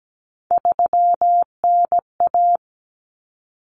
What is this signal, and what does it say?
Morse code, heard as a keyed tone, about 17 wpm, 700 Hz. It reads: 3NA
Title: Common Cut Numbers - 3-Digit 17wpm